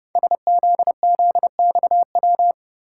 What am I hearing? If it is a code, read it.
Morse code, heard as a keyed tone, about 30 wpm, 700 Hz. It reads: SZZXW
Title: Code Groups - 5 Letters - 30wpm